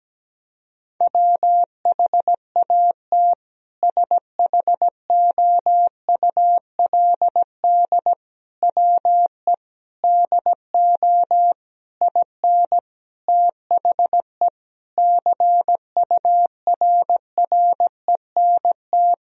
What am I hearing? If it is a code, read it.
Morse code, heard as a keyed tone, about 17 wpm, 700 Hz. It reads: WHAT SHOULD WE DO IN THE CURRENT
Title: Sentences from Top 1000 Words - Review Part 4 - 17wpm